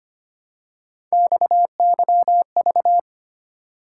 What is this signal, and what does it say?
Morse code, heard as a keyed tone, about 25 wpm, 700 Hz. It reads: XYV
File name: Three Letter-Number Combinations - 25wpm